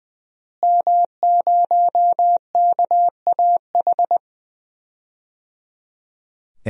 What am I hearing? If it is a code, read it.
Morse code, heard as a keyed tone, about 20 wpm, 700 Hz. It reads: M0KAH
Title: DXCC-100 Call Signs - 5 Characters - Repeated 3x - 20wpm